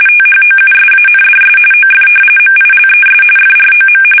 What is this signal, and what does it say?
RTTY, about 45.45 baud. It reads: THOUSANDS OF MILES FROM
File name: fsk850 / battleshipnj